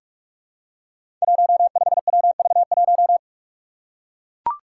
Morse code, heard as a keyed tone, about 44 wpm, 700 Hz. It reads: 15WV1
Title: Five Challenging Letter-Number Combinations 45wpm